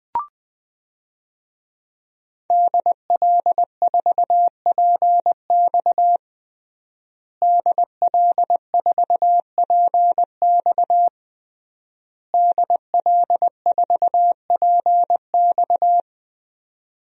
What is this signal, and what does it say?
Morse code, heard as a keyed tone, about 20 wpm, 700 Hz. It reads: DL4PX DL4PX DL4PX
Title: DXCC-100 Call Signs - 5 Characters - Repeated 3x - 20wpm